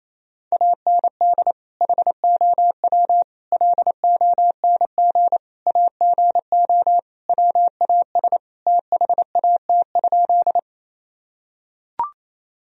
Morse code, heard as a keyed tone, about 28 wpm, 700 Hz. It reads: AND HOW LONG AGO WAS THAT?
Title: Sentences from Top 1000 Words - Review Part 3 - 28wpm